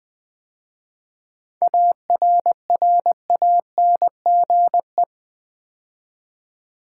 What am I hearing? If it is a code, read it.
Morse code, heard as a keyed tone, about 20 wpm, 690 Hz. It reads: ARRANGE
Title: 7 Letter Words - First 3 Letters Given - 20wpm